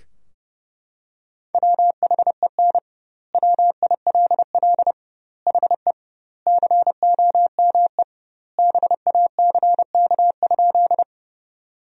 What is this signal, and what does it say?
Morse code, heard as a keyed tone, about 30 wpm, 700 Hz. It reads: WHEN WILL HE COME BACK?
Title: 2x Word-Spacing - Sentences from Top 200 Words - 30wpm